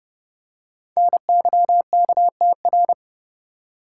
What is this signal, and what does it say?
Morse code, heard as a keyed tone, about 29 wpm, 690 Hz. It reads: NYKTR
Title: Code Groups - 5 Letters - 30wpm